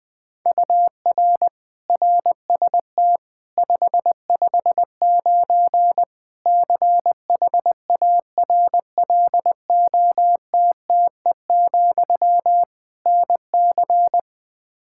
Morse code, heard as a keyed tone, about 20 wpm, 700 Hz. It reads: UR RST 559 CHARLOTTE, NC
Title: Practice QSO - Exchange 1 - Beginning - 20wpm